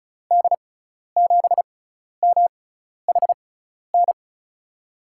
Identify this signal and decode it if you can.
Morse code, heard as a keyed tone, about 36 wpm, spaced as for 12 wpm, 700 Hz. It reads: D7MHN
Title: USA Call Signs - 6 Characters - 6x Character Spacing - 35wpm